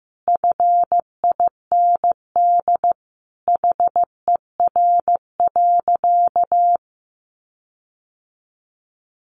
Morse code, heard as a keyed tone, about 15 wpm, 700 Hz. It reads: FIND HER.
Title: Sentences from Top 1300 Words 15wpm